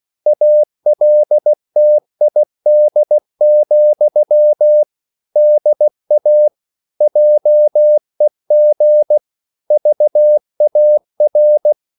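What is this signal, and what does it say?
Morse code, heard as a keyed tone, about 16 wpm, 600 Hz. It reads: ALTID, DA JEG VAR
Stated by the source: Slow (QRS) Morse code at 16 wpm, from CW med Gnister.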